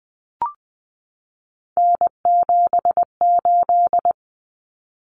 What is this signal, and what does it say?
Morse code, heard as a keyed tone, about 20 wpm, 700 Hz. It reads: N78
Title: Common Cut Numbers - 3-Digit 20wpm